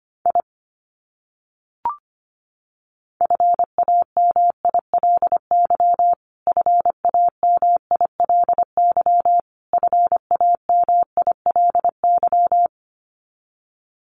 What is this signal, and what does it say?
Morse code, heard as a keyed tone, about 25 wpm, 700 Hz. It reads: I FAMILY FAMILY FAMILY
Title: Top 500 Words Repeated 25wpm